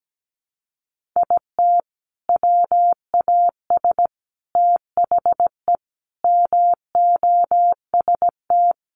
Morse code, heard as a keyed tone, about 17 wpm, 700 Hz. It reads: IT WAS THE MOST
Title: Sentences from Top 1400 Words 17wpm